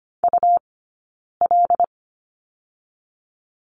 Morse code, heard as a keyed tone, about 25 wpm, 700 Hz. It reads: U L
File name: Lesson 17d - Letter u - Two Characters 25wpm